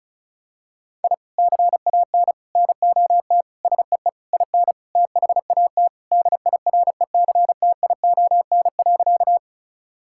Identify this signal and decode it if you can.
Morse code, heard as a keyed tone, about 36 wpm, 700 Hz. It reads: I CAN NOT SEE IN THAT DIRECTION.
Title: Sentences from Top 1000 Words - Review Part 5 - 35wpm